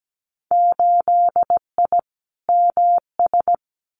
Morse code, heard as a keyed tone, about 17 wpm, 700 Hz. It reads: 8I MS
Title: Winter Field Day Exchanges Encore 17wpm